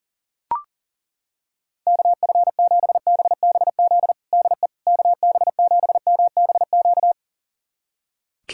Morse code, heard as a keyed tone, about 40 wpm, 700 Hz. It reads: KF7BBZ DE KB7MBQ
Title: USA Callsign DE Callsign - 6 Characters - 40wpm